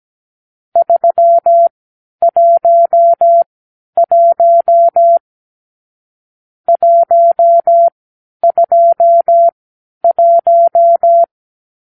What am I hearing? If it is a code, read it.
Morse code, heard as a keyed tone, about 17 wpm, spaced as for 10 wpm, 680 Hz. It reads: 311 121
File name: k7qo / Lesson 44 (1-3)